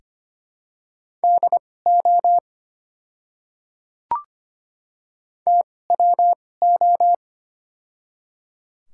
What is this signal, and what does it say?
Morse code, heard as a keyed tone, about 25 wpm, spaced as for 18 wpm, 700 Hz. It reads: DO TWO
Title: 2x Character Spacing - Top 200 Words - 25wpm